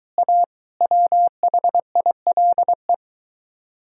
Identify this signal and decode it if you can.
Morse code, heard as a keyed tone, about 23 wpm, 700 Hz. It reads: A WHILE